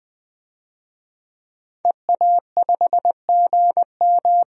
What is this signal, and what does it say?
Morse code, heard as a keyed tone, about 20 wpm, 700 Hz. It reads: EA5GM
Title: DXCC-100 Call Signs - 5 Characters - Repeated 2x - 20wpm